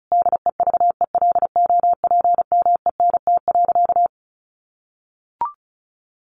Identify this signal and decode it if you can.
Morse code, heard as a keyed tone, about 34 wpm, 700 Hz. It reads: DEVELOPMENT.
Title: Sentences from Top 1400 Words 35wpm